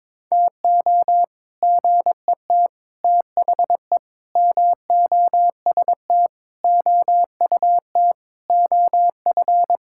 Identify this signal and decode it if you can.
Morse code, heard as a keyed tone, about 22 wpm, 700 Hz. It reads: TO GET THE MOST OUT OF
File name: practice / 22wpm / Sets of 7 Words - 22wpm